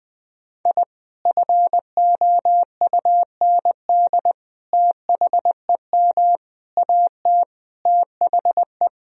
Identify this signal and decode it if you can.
Morse code, heard as a keyed tone, about 21 wpm, 700 Hz. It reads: I FOUND THEM AT THE
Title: Sentences from Top 1000 Words - Review Part 2 - 20wpm